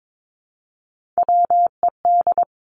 Morse code, heard as a keyed tone, about 22 wpm, 700 Hz. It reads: WED
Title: Three Challenging Letter-Number Combinations 22wpm